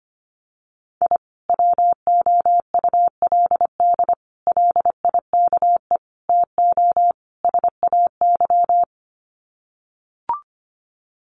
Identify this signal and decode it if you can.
Morse code, heard as a keyed tone, about 25 wpm, 700 Hz. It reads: I WOULD LIKE TO SAY
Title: Sets of 5 Words Encore2 25wpm